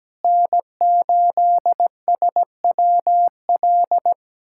Morse code, heard as a keyed tone, about 17 wpm, 700 Hz. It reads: N8SWL
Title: USA Callsign DE Callsign 17wpm